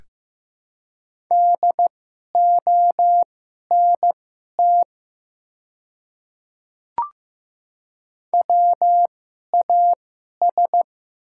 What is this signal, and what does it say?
Morse code, heard as a keyed tone, about 15 wpm, spaced as for 11 wpm, 700 Hz. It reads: DONT WAS
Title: 2x Character Spacing - Top 300 Words - 15wpm